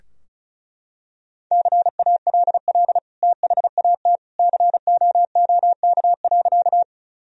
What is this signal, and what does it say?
Morse code, heard as a keyed tone, about 34 wpm, 700 Hz. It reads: CALL THAT COOK.
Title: Sentences from Top 1000 Words - Review Part 4 - 35wpm